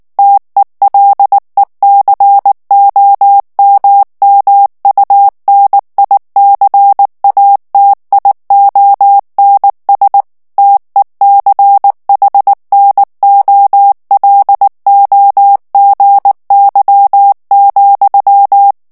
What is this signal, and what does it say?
Morse code, heard as a keyed tone, about 19 wpm, 800 Hz. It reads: TELECOMMUNICATIONS TECHNOLOGY,
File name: ANGLAIS texte clair 3 vitesse 18 mots minute